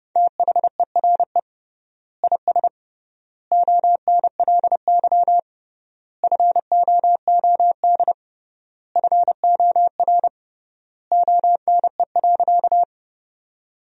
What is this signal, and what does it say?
Morse code, heard as a keyed tone, about 32 wpm, 700 Hz. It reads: THERE IS ONLY FOOD FOR ONE.
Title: 3x Word-Spacing - Sentences from Top 300 Words - 30wpm